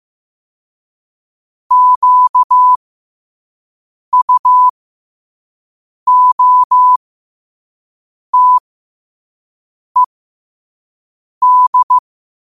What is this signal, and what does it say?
Morse code, heard as a keyed tone, about 15 wpm, spaced as for 5 wpm, 1000 Hz. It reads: QUOTED
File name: qotdfiles / 5wpm